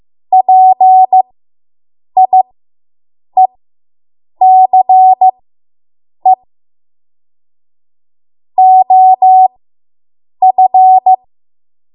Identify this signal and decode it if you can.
Morse code, heard as a keyed tone, about 15 wpm, spaced as for 7 wpm, 750 Hz. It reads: PIECE OF